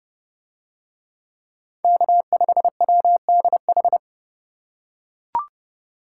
Morse code, heard as a keyed tone, about 29 wpm, 700 Hz. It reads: K5WDH